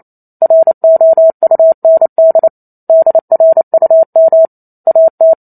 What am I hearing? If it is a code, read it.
Morse code, heard as a keyed tone, about 28 wpm, 650 Hz. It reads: ROUND DRUM AT